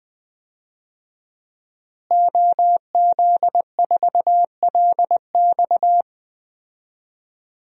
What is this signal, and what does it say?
Morse code, heard as a keyed tone, about 21 wpm, 700 Hz. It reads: OZ4LX
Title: DXCC-100 Call Signs - 5 Characters - Repeated 2x - 20wpm